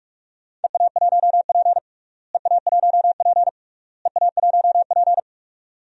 Morse code, heard as a keyed tone, about 46 wpm, 700 Hz. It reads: EA1P EA1P EA1P
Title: DXCC-100 Call Signs - 4 Characters - Repeated 3x - 45wpm